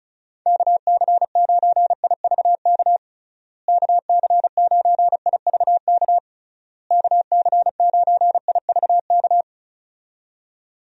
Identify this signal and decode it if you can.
Morse code, heard as a keyed tone, about 36 wpm, 700 Hz. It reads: KC9IVK KC9IVK KC9IVK
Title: USA Call Signs - 6 Characters - Repeated 3x - 35wpm